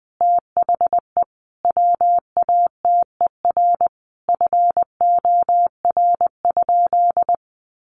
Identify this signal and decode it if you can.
Morse code, heard as a keyed tone, about 20 wpm, 700 Hz. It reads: THE WATER FOR?
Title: Sentences from Top 1300 Words 20wpm